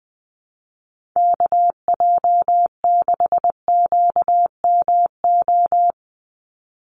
Morse code, heard as a keyed tone, about 20 wpm, 700 Hz. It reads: KJ6QMO
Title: USA Call Signs 20wpm